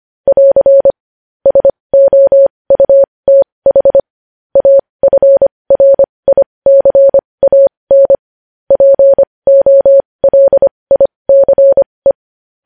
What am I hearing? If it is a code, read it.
Morse code, heard as a keyed tone, about 25 wpm, spaced as for 20 wpm, 550 Hz. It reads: <AR> SOUTH AFRICAN POLICE
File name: podcast / News headlines-1x-25x20